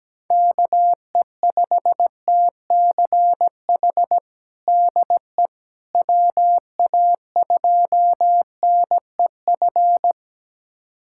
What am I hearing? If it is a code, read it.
Morse code, heard as a keyed tone, about 17 wpm, 700 Hz. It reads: KE5TCH DE WA2NEF
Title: USA Callsign DE Callsign 17wpm